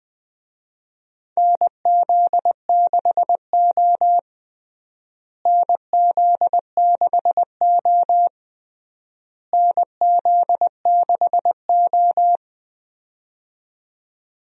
Morse code, heard as a keyed tone, about 20 wpm, 700 Hz. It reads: NZ6O NZ6O NZ6O
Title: USA Call Signs - 4 Characters - Repeated 3x - 20wpm